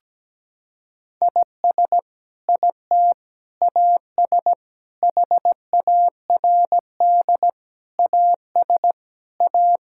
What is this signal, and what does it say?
Morse code, heard as a keyed tone, about 17 wpm, 700 Hz. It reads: IS IT AS HARD AS A
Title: Sentences from Top 1600 Words 17wpm